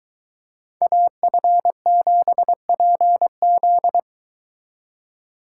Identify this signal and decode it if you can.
Morse code, heard as a keyed tone, about 23 wpm, 700 Hz. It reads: AF7PZ